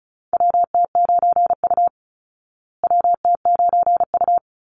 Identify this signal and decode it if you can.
Morse code, heard as a keyed tone, about 34 wpm, 700 Hz. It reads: WT9U WT9U
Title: Call Signs Repeated 3x - 35wpm